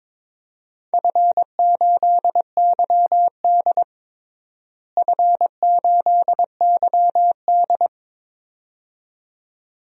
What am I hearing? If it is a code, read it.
Morse code, heard as a keyed tone, about 21 wpm, 700 Hz. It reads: F8YD F8YD
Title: DXCC-100 Call Signs - 4 Characters - Repeated 2x - 22wpm